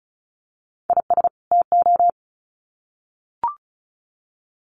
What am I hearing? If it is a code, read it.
Morse code, heard as a keyed tone, about 34 wpm, 700 Hz. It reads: IS TO
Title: Sets of 2 Words 35wpm